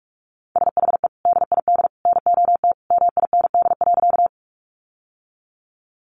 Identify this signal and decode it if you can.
Morse code, heard as a keyed tone, about 44 wpm, 700 Hz. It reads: SHE DID NOT MIND.
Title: Sentences from Top 600 Words 45wpm